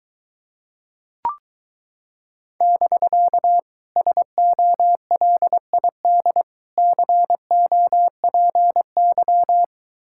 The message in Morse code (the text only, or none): <BK> SOLID COPY